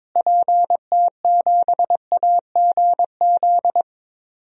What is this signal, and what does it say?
Morse code, heard as a keyed tone, about 22 wpm, 700 Hz. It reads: PT7AGZ
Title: DXCC-100 Call Signs - 6 Characters - Repeated 3x - 22wpm